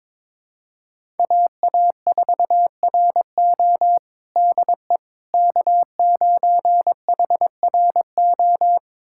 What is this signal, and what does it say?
Morse code, heard as a keyed tone, about 22 wpm, 700 Hz. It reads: AA4RO DE K9HRO
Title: USA Callsign DE Callsign - 5 Characters - 22wpm